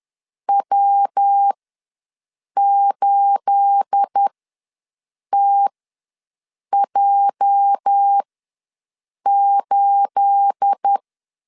Morse code, heard as a keyed tone, about 10 wpm, spaced as for 6 wpm, 800 Hz. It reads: W8TJ8
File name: LEZIONI / 9 28 ws 10wpm 4124